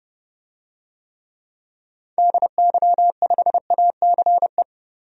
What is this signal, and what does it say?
Morse code, heard as a keyed tone, about 31 wpm, 700 Hz. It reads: DY5ACE
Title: DXCC-100 Call Signs - 6 Characters - Repeated 2x - 30wpm